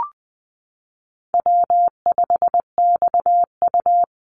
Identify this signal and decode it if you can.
Morse code, heard as a keyed tone, about 20 wpm, 700 Hz. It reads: W5XU
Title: Easy Call Signs Repeated 3x - 20wpm